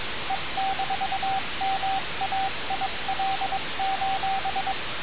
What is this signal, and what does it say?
Morse code, heard as a keyed tone, about 22 wpm, 770 Hz, with radio noise. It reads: E-MAIL:
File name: radiocap / REC z